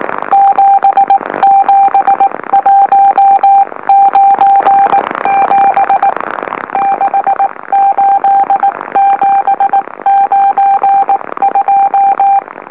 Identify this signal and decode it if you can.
Morse code, heard as a keyed tone, about 19 wpm, 790 Hz. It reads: 77197 68792
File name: M04- -2345utc-LOLO